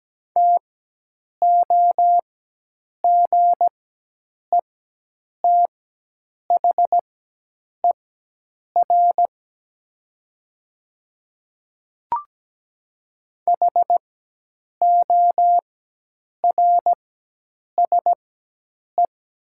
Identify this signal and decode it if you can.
Morse code, heard as a keyed tone, about 17 wpm, spaced as for 8 wpm, 700 Hz. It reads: TOGETHER HORSE
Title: 4x Character Spacing - Top 500 Words - 17wpm